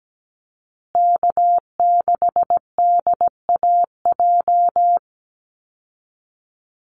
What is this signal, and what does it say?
Morse code, heard as a keyed tone, about 17 wpm, 700 Hz. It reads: K6DAJ
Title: Call Signs Repeated 2x - 17wpm